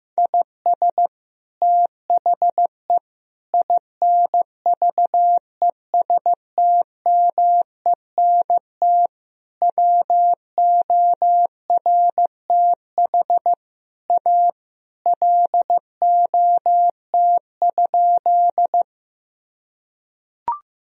Morse code, heard as a keyed tone, about 15 wpm, 700 Hz. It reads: IS THE INVESTMENT WORTH A LOT?
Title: Sentences from Top 1000 Words - Review Part 4 - 15wpm